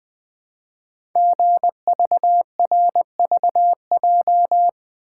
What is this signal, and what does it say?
Morse code, heard as a keyed tone, about 20 wpm, 700 Hz. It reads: GVRVJ
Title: Code Groups - 5 Letter:Numbers - 20wpm